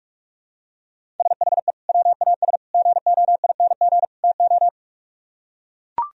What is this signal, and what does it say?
Morse code, heard as a keyed tone, about 44 wpm, 700 Hz. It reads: SHE WAS GOING TO